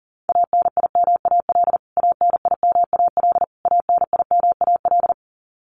Morse code, heard as a keyed tone, about 40 wpm, 700 Hz. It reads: ANIMAL ANIMAL ANIMAL
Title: Top 500 Words Repeated 40wpm